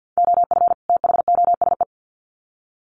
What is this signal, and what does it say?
Morse code, heard as a keyed tone, about 48 wpm, 700 Hz. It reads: OF THOSE
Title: Sets of 2 Words 50wpm